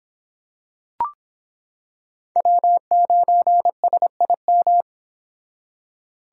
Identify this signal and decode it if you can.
Morse code, heard as a keyed tone, about 25 wpm, 700 Hz. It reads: W9SIM